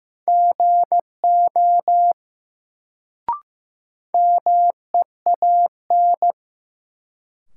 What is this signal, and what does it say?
Morse code, heard as a keyed tone, about 16 wpm, 700 Hz. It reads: GO MEAN
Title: Sets of 2 Words - New Word Drill - 15wpm